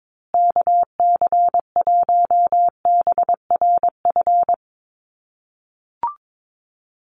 Morse code, heard as a keyed tone, about 22 wpm, 700 Hz. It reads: KC1BRF
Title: USA Call Signs Repeated 2x 22wpm